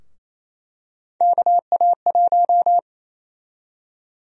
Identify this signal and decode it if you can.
Morse code, heard as a keyed tone, about 28 wpm, 700 Hz. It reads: KA1
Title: USA Call Sign Prefixes - Repeated 2x - 28wpm